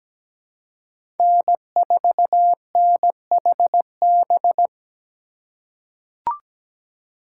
Morse code, heard as a keyed tone, about 17 wpm, 700 Hz. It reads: N4NHB
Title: USA Call Signs - 5 Characters - 3x Character Spacing - 17wpm